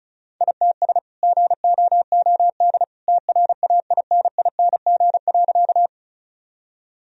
Morse code, heard as a keyed tone, about 34 wpm, 700 Hz. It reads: ITS GOOD TRAINING.
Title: Sentences from Top 1000 Words - Review Part 4 - 35wpm